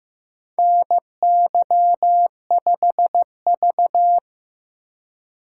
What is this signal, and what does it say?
Morse code, heard as a keyed tone, about 15 wpm, 700 Hz. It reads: NY5V
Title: USA Call Signs - 4 Characters - 2x Character Spacing - 15wpm